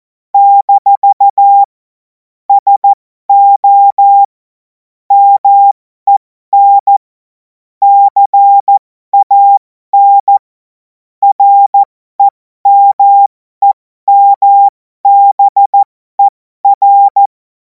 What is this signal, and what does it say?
Morse code, heard as a keyed tone, about 14 wpm, spaced as for 12 wpm, 800 Hz. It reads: - SO MEN CAN REMEMBER
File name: Morsekurs / Witze